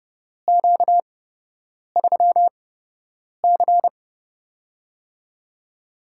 Morse code, heard as a keyed tone, about 30 wpm, spaced as for 8 wpm, 700 Hz. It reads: Q3C
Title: USA Call Signs - 4 Characters - 8x Character Spacing - 30wpm